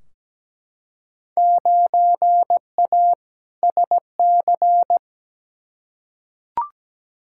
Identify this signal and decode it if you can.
Morse code, heard as a keyed tone, about 17 wpm, 700 Hz. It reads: 9A SC
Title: Field Day Exchanges Encore - 17wpm